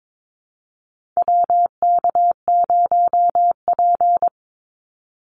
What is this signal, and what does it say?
Morse code, heard as a keyed tone, about 22 wpm, 700 Hz. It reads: WK0P
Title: Call Signs - 22wpm